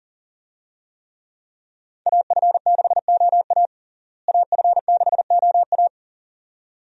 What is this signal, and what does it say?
Morse code, heard as a keyed tone, about 40 wpm, 700 Hz. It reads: AF6OA AF6OA
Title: USA Call Signs - 5 Characters - Repeated 2x - 40wpm